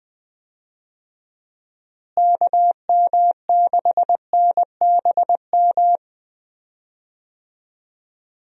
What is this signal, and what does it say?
Morse code, heard as a keyed tone, about 21 wpm, 700 Hz. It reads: KM6NBM
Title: USA Call Signs - 6 Characters - Repeated 2x - 20wpm